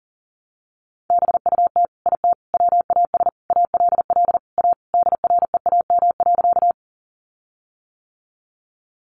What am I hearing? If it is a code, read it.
Morse code, heard as a keyed tone, about 40 wpm, 700 Hz. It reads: BUT IT WAS ALL A DREAM.
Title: Sentences from Top 1500 Words 40wpm